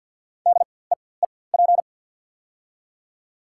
Morse code, heard as a keyed tone, about 47 wpm, spaced as for 23 wpm, 690 Hz. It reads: DEEP
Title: 4x Character Spacing - Top 500 Words - 50wpm